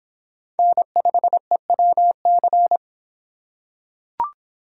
Morse code, heard as a keyed tone, about 25 wpm, 700 Hz. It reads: N5EWC